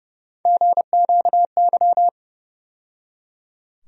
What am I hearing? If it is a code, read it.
Morse code, heard as a keyed tone, about 30 wpm, 700 Hz. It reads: GQY
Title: USA Call Sign Postfixes - 3 Letters 30wpm